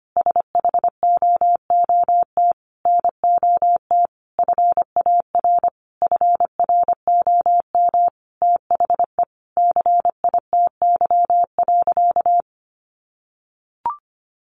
Morse code, heard as a keyed tone, about 25 wpm, 700 Hz. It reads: SHOOT NOT FAR FROM THE CITY.
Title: Sentences from Top 1200 Words 25wpm